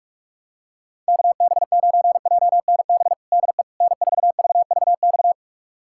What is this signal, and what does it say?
Morse code, heard as a keyed tone, about 46 wpm, 700 Hz. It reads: KB9JNB DE N4VVX
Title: USA Callsign DE Callsign 45wpm